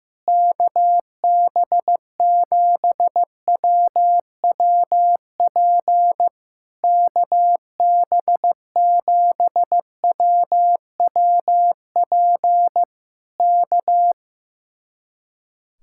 Morse code, heard as a keyed tone, about 15 wpm, 700 Hz. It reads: KB7WWP KB7WWP K
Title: Speed-Racing Common QSO Phrases Encore - 15wpm